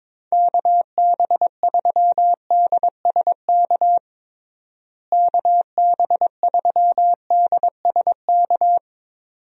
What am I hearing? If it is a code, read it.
Morse code, heard as a keyed tone, about 22 wpm, 700 Hz. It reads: KB3DSK KB3DSK
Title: USA Call Signs - 6 Characters - Repeated 2x - 22wpm